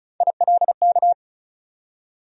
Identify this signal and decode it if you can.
Morse code, heard as a keyed tone, about 34 wpm, 700 Hz. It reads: ILK
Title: USA Call Sign Postfixes - 3 Letters 35wpm